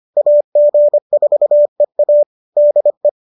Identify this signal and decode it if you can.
Morse code, heard as a keyed tone, about 25 wpm, 590 Hz. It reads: AG4EA DE
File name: qso206 25